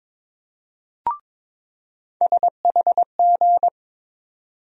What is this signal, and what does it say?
Morse code, heard as a keyed tone, about 22 wpm, 700 Hz. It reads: SHG
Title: USA Call Sign Postfixes - 3 Letters 22wpm